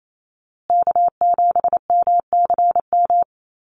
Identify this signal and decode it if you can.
Morse code, heard as a keyed tone, about 28 wpm, 700 Hz. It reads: K7MCM
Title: USA Call Signs Repeated 3x 28wpm